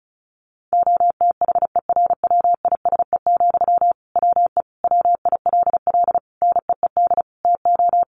Morse code, heard as a keyed tone, about 34 wpm, 700 Hz. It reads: OTHERWISE, WE WILL NEED TO
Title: Sentences from Top 1500 Words 35wpm